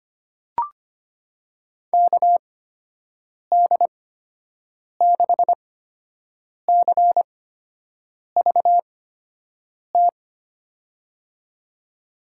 Morse code, heard as a keyed tone, about 25 wpm, spaced as for 7 wpm, 700 Hz. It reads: KD6CVT
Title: USA Call Signs - 6 Characters - 8x Character Spacing - 25wpm